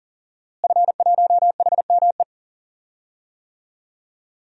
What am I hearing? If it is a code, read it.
Morse code, heard as a keyed tone, about 40 wpm, 700 Hz. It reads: F1HME
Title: DXCC-100 Call Signs - 5 Characters - Repeated 3x - 40wpm